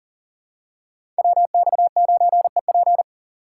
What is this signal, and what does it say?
Morse code, heard as a keyed tone, about 40 wpm, 700 Hz. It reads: WX9EP
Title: USA Call Signs - 5 Characters - 3x Character Spacing - 40wpm